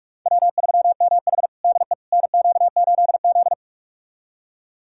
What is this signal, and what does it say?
Morse code, heard as a keyed tone, about 46 wpm, 700 Hz. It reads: W3MH DE NQ8Z
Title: USA Callsign DE Callsign - 4 Characters - 45wpm